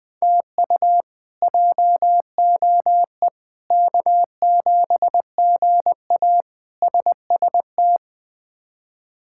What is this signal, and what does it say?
Morse code, heard as a keyed tone, about 20 wpm, 700 Hz. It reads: TU JOE K7GA SST
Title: K1USN SST QSO Practice - 20wpm